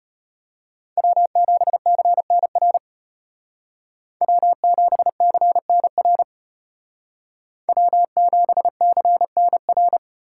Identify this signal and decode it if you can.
Morse code, heard as a keyed tone, about 32 wpm, 700 Hz. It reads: W7CNR W7CNR W7CNR